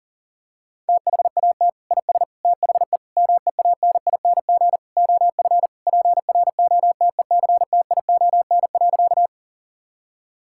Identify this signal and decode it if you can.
Morse code, heard as a keyed tone, about 37 wpm, 700 Hz. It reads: THAT IS THE MEANING OF PROTECTION.
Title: Sentences from Top 1600 Words 40wpm